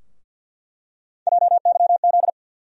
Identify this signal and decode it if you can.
Morse code, heard as a keyed tone, about 46 wpm, 700 Hz. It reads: JYZ
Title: USA Call Sign Postfixes - 3 Letters 50wpm